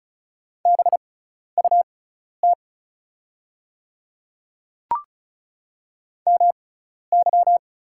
Morse code, heard as a keyed tone, about 36 wpm, spaced as for 12 wpm, 700 Hz. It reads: BUT MY
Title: 6x Character Spacing - Top 200 Words - 35wpm